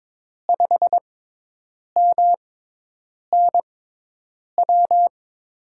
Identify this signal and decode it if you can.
Morse code, heard as a keyed tone, about 22 wpm, spaced as for 8 wpm, 700 Hz. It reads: 5MNW
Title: USA Call Signs - 6 Characters - 6x Character Spacing - 22wpm